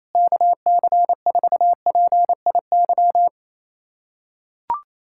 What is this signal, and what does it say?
Morse code, heard as a keyed tone, about 28 wpm, 700 Hz. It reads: KC4PIY
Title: USA Callsign DE Callsign - 6 Characters - 28wpm